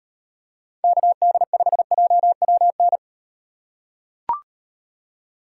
Morse code, heard as a keyed tone, about 34 wpm, 700 Hz. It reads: KD5JWN